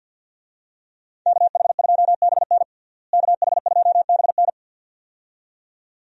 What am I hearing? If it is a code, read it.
Morse code, heard as a keyed tone, about 51 wpm, 700 Hz. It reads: KH2BN KH2BN
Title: USA Call Signs - 5 Characters - Repeated 2x - 50wpm